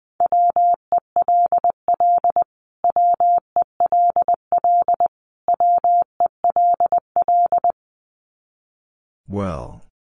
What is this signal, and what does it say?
Morse code, heard as a keyed tone, about 20 wpm, 700 Hz. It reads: WELL WELL WELL
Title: Top 200 Words Repeated 20wpm